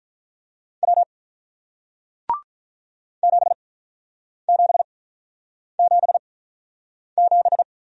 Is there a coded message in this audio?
Yes: Morse code from 0.8 to 7.6 s, heard as a keyed tone, about 47 wpm, spaced as for 9 wpm, 700 Hz. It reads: W 7777